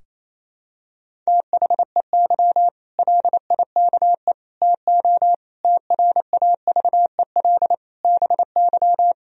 Morse code, heard as a keyed tone, about 28 wpm, 700 Hz. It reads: THEY LIKE TO TRAVEL BY
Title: Sentences from Top 1000 Words - Review Part 3 - 28wpm